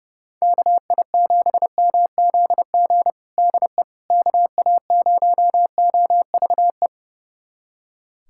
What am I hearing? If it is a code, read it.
Morse code, heard as a keyed tone, about 32 wpm, 700 Hz. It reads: KI7MZG DE KA0OVE
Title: USA Callsign DE Callsign 30wpm